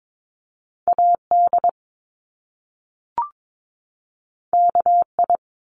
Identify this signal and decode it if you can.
Morse code, heard as a keyed tone, about 22 wpm, 700 Hz. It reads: AD KI
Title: Top Two Letter Combinations in Words 22wpm